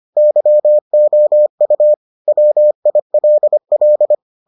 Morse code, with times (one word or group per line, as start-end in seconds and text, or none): 0.2-1.9 YOU
2.3-4.1 WILL